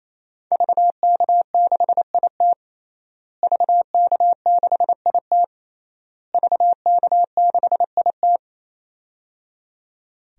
Morse code, heard as a keyed tone, about 27 wpm, 700 Hz. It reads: VK6IT VK6IT VK6IT
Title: DXCC-100 Call Signs - 5 Characters - Repeated 3x - 28wpm